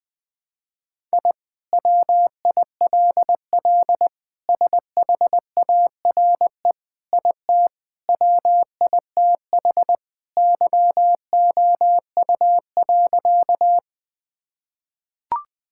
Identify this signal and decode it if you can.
Morse code, heard as a keyed tone, about 21 wpm, 700 Hz. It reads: I WILL SHARE IT WITH YOU.
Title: Sentences from Top 1000 Words - Review Part 6 - 20wpm